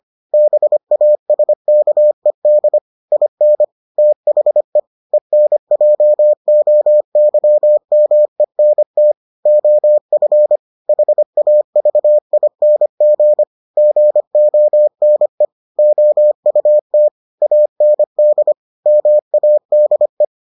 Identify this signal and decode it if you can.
Morse code, heard as a keyed tone, about 25 wpm, 600 Hz. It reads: BASKED IN THE ENJOYMENT OF HAVING GONE OUT AND MADE